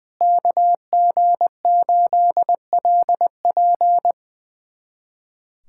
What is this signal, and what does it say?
Morse code, heard as a keyed tone, about 21 wpm, 700 Hz. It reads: KG8LP
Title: USA Callsign DE Callsign 20wpm